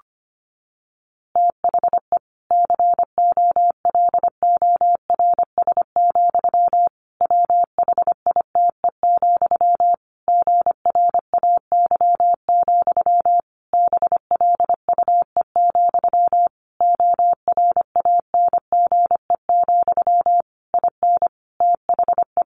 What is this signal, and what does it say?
Morse code, heard as a keyed tone, about 25 wpm, 700 Hz. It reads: THE COLORS, WHITE, GRAY, BLUE, ORANGE, IN THE
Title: A Sound Of Thunder 25wpm